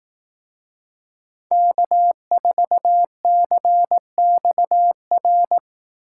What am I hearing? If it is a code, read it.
Morse code, heard as a keyed tone, about 18 wpm, 700 Hz. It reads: K4CXR
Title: Speed Racing - USA 1x3 Call Signs 15wpm